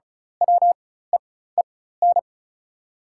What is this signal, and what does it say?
Morse code, heard as a keyed tone, about 34 wpm, spaced as for 16 wpm, 700 Hz. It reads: WEEN
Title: 4x Character Spacing - Top 500 Words - 35wpm